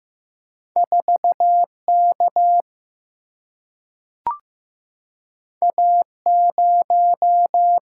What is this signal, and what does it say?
Morse code, heard as a keyed tone, about 15 wpm, 700 Hz. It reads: 4K A0
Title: Single-Letter Single-Number 15wpm